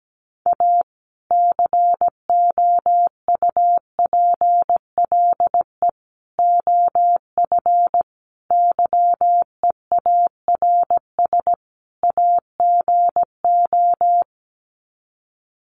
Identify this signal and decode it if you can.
Morse code, heard as a keyed tone, about 17 wpm, 700 Hz. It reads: A COUPLE OF YEARS AGO
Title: Sets of 5 Words 17wpm